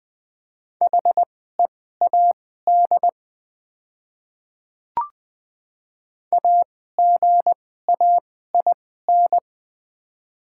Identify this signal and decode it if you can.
Morse code, heard as a keyed tone, about 20 wpm, spaced as for 15 wpm, 700 Hz. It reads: HEAD AGAIN
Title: 2x Character Spacing - Top 300 Words - 20wpm